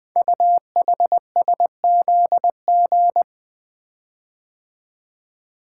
Morse code, heard as a keyed tone, about 20 wpm, 700 Hz. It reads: UHSZG
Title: Code Groups - 5 Letters - 20wpm